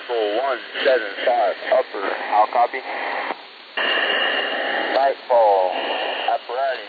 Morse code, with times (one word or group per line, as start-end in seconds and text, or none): none